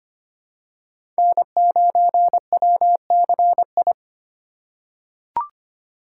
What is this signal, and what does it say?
Morse code, heard as a keyed tone, about 25 wpm, 700 Hz. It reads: N9WCI
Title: USA Call Signs - 5 Characters - 3x Character Spacing - 25wpm